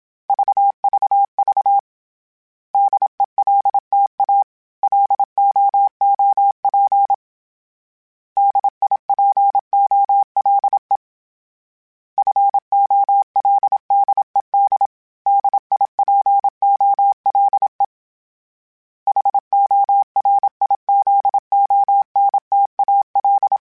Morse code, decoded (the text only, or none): VVV DELTA LOOP DIPOLE FOLDED DIPOLE HORIZONTAL